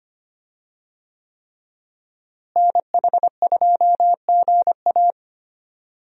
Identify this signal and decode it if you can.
Morse code, heard as a keyed tone, about 25 wpm, 700 Hz. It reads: NH2GA
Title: USA Call Signs - 5 Characters - Repeated 3x - 25wpm